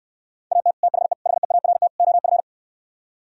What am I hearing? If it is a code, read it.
Morse code, heard as a keyed tone, about 69 wpm, 700 Hz. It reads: AT THE HEART OF